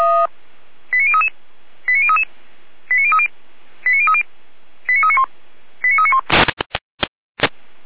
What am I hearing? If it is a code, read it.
Five-tone selective call (ZVEI), heard as quick runs of five tones, tones of about 70 ms. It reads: 89033 89033 89030 89030 89391 89391